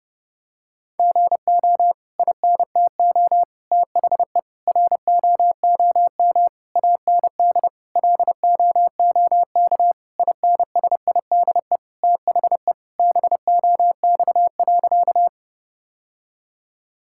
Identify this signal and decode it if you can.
Morse code, heard as a keyed tone, about 30 wpm, 700 Hz. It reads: GO INTO THE ROOM AND LOOK INSIDE THE BOX.
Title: Sentences from Top 1000 Words - Review Part 6 - 30wpm